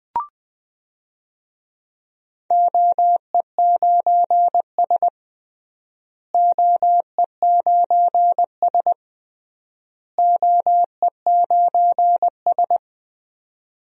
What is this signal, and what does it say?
Morse code, heard as a keyed tone, about 20 wpm, 700 Hz. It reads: OE9S OE9S OE9S
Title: DXCC-100 Call Signs - 4 Characters - Repeated 3x - 20wpm